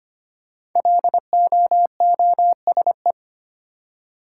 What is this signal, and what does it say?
Morse code, heard as a keyed tone, about 25 wpm, 700 Hz. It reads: LOOSE
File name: 5 Letter Words - First 3 Letters Given - 25wpm